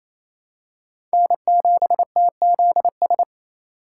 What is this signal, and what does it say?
Morse code, heard as a keyed tone, about 28 wpm, 700 Hz. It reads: N7TZS
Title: USA Call Signs - 5 Characters - 4x Character Spacing - 28wpm